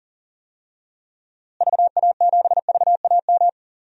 Morse code, heard as a keyed tone, about 40 wpm, 700 Hz. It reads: VA7VAM
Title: DXCC-100 Call Signs - 6 Characters - Repeated 2x - 40wpm